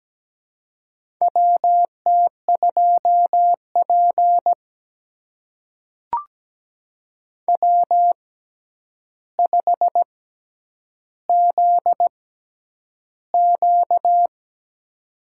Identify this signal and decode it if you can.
Morse code, heard as a keyed tone, about 17 wpm, spaced as for 9 wpm, 700 Hz. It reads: WT2P W5ZQ
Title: USA Call Signs - 4 Characters - 6x Character Spacing - 17wpm